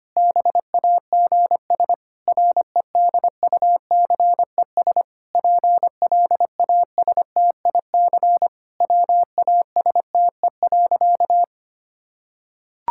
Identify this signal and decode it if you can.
Morse code, heard as a keyed tone, about 25 wpm, 700 Hz. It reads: BAGS REDUCES PLASTIC WASTE.
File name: practice / 25wpm / Difficult Sentences with Hints - 25wpm